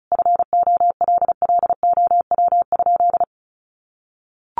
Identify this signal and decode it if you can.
Morse code, heard as a keyed tone, about 34 wpm, 700 Hz. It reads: FOLLOW?
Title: Sentences from Top 1100 Words 35wpm